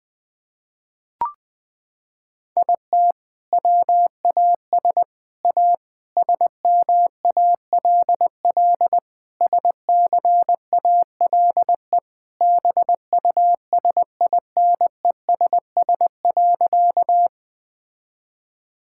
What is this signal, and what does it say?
Morse code, heard as a keyed tone, about 21 wpm, 700 Hz. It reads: IT WAS A SMALL SCALE BUSINESS.
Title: Sentences from Top 1000 Words - Review Part 5 - 20wpm